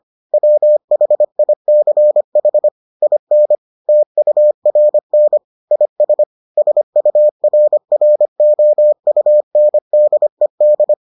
Morse code, heard as a keyed tone, about 25 wpm, 600 Hz. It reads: WHICH IN TURN IS SURROUNDED